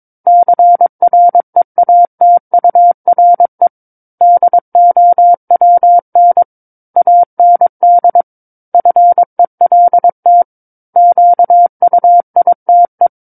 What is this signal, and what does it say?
Morse code, heard as a keyed tone, about 22 wpm, 700 Hz. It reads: CREATURE DOWN AND FELT QUITE
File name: 06-Chapter-06 22 WPM